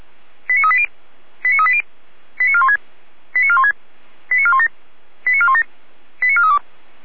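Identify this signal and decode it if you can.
Five-tone selective call (ZVEI), heard as quick runs of five tones, tones of about 70 ms. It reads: 89380 89380 89416 89416 89417 89417 89432
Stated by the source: ZVEI five-tone selective call, address 89127, received by SDR at feuerwehr-orlamuende (fire brigade Orlamünde).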